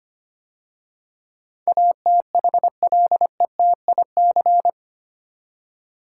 Morse code, heard as a keyed tone, about 25 wpm, 700 Hz. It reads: ATHLETIC
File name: 8 Letter Words - First 3 Letters Given - 25wpm